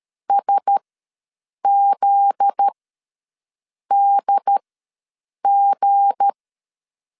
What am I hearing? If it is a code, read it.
Morse code, heard as a keyed tone, about 13 wpm, spaced as for 6 wpm, 800 Hz. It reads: SZDG
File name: 7 GF9Z ws 12wpm